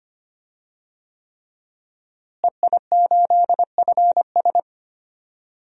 Morse code, heard as a keyed tone, about 25 wpm, 700 Hz. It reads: EI8FS
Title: DXCC-100 Call Signs - 5 Characters - Repeated 3x - 25wpm